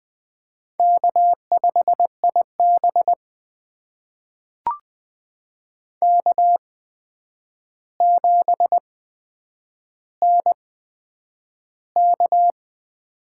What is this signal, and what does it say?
Morse code, heard as a keyed tone, about 20 wpm, spaced as for 9 wpm, 700 Hz. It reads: K5IB K7NK
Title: USA Call Signs - 4 Characters - 8x Character Spacing - 20wpm